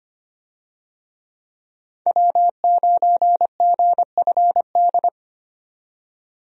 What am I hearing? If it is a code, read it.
Morse code, heard as a keyed tone, about 25 wpm, 700 Hz. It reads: W9GFD
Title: USA Call Signs - 5 Characters - 25wpm